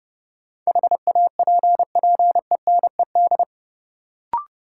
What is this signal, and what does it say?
Morse code, heard as a keyed tone, about 30 wpm, 700 Hz. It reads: HAPPENED
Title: Sentences from Top 1000 Words - Review Part 2 - New Word Drill - 30wpm